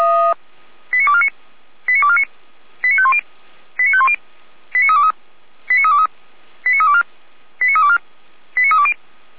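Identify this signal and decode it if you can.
Five-tone selective call (ZVEI), heard as quick runs of five tones, tones of about 70 ms. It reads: 89249 89249 89510 89510 89323 89323 89324 89324 89320
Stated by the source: Received by SDR, associated with feuerwehr-orlamuende.org (fire brigade Orlamünde).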